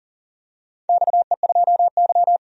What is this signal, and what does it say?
Morse code, heard as a keyed tone, about 40 wpm, 700 Hz. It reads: XE2Y
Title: DXCC-100 Call Signs - 4 Characters - 40wpm